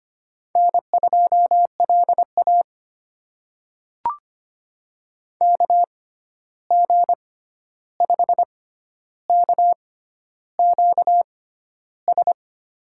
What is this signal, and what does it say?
Morse code, heard as a keyed tone, about 25 wpm, spaced as for 11 wpm, 700 Hz. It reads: N2LA KG5KQS
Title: USA Call Signs - 6x Character Spacing - 25wpm